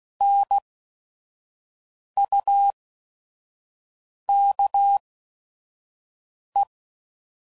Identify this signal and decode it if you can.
Morse code, heard as a keyed tone, about 16 wpm, spaced as for 5 wpm, 800 Hz. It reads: NUKE
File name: Prefixes / By First Letter / Ns